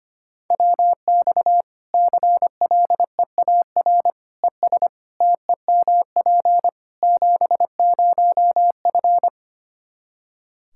Morse code, heard as a keyed tone, about 25 wpm, 700 Hz. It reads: WX CLEAR ES TEMP 70F
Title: Practice QSO - Exchange 2 - Info - 25wpm